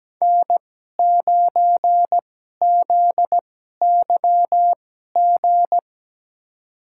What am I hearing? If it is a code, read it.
Morse code, heard as a keyed tone, about 17 wpm, spaced as for 12 wpm, 700 Hz. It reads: N9ZYG
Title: USA Call Signs - 5 Characters - 2x Character Spacing - 17wpm